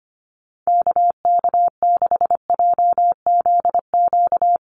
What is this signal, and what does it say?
Morse code, heard as a keyed tone, about 25 wpm, 700 Hz. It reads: KK6JZQ
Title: USA Call Signs Repeated 2x 25wpm